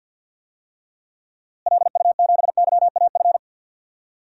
Morse code, heard as a keyed tone, about 52 wpm, 700 Hz. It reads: LU7YAF
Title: DXCC-100 Call Signs - 6 Characters - Repeated 3x - 50wpm